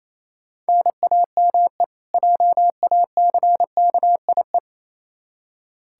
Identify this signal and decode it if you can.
Morse code, heard as a keyed tone, about 28 wpm, 700 Hz. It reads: NAME JACKIE
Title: K1USN SST QSO Practice 28wpm